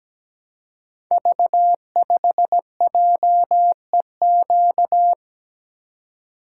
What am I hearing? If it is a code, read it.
Morse code, heard as a keyed tone, about 17 wpm, 700 Hz. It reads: V5JEQ
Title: Code Groups - 5 Letter:Numbers - 17wpm